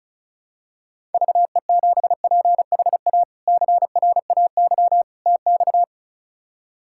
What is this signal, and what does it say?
Morse code, heard as a keyed tone, about 34 wpm, 700 Hz. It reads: VE7PHA CRAY TX
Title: K1USN SST QSO Practice - 35wpm